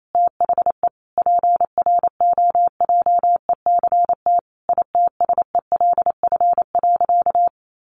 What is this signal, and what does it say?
Morse code, heard as a keyed tone, about 28 wpm, 700 Hz. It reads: THE PROJECT ITSELF.
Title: Sentences from Top 1400 Words 28wpm